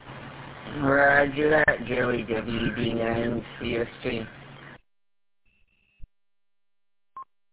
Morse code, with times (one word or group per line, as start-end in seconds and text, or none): none